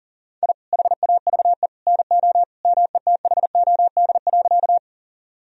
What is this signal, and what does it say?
Morse code, heard as a keyed tone, about 40 wpm, 700 Hz. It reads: I HAVE NO METHOD.
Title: Sentences from Top 1000 Words - Review Part 3 - 40wpm